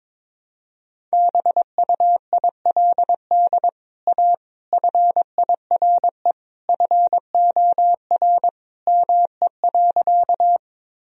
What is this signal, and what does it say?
Morse code, heard as a keyed tone, about 22 wpm, 700 Hz. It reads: BUILD A FIRE FOR ME.
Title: Sentences from Top 1000 Words - Review Part 1 - 22wpm